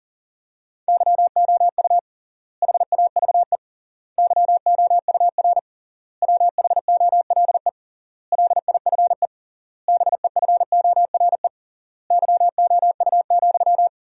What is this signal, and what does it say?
Morse code, heard as a keyed tone, about 40 wpm, 700 Hz. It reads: YOU HAVE YOUR WHOLE LIFE BEFORE YOU,
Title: 3x Word-Spacing - Sentences from Top 400 Words - 40wpm